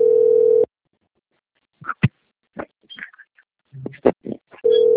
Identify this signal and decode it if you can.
Ringback tone (North America), 440 + 480 Hz.
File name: out- - - - - .71555